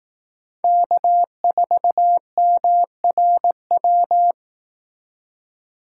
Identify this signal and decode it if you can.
Morse code, heard as a keyed tone, about 18 wpm, 700 Hz. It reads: K4MRW
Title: Speed Racing - USA Call Signs 15wpm